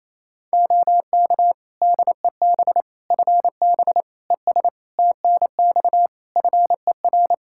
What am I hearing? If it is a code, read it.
Morse code, heard as a keyed tone, about 27 wpm, 700 Hz. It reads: OK DEB FB ES TNX FER
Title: Practice QSO - Exchange 2 - Info - 28wpm